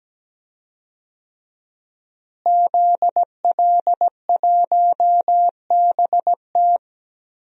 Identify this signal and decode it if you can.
Morse code, heard as a keyed tone, about 17 wpm, 700 Hz. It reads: ZL1BT
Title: DXCC-100 Call Signs - 5 Characters - Repeated 2x - 17wpm